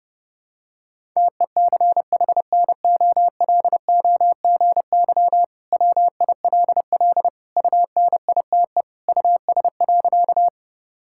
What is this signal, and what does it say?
Morse code, heard as a keyed tone, about 32 wpm, 700 Hz. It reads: TECHNOLOGY WILL UNITE US.
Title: Sentences from Top 1000 Words - Review Part 5 - 30wpm